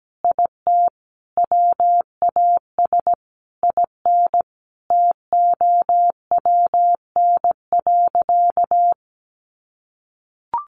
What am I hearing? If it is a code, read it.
Morse code, heard as a keyed tone, about 17 wpm, 700 Hz. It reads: IT WAS IN TOWN.
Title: Sentences from Top 1500 Words 17wpm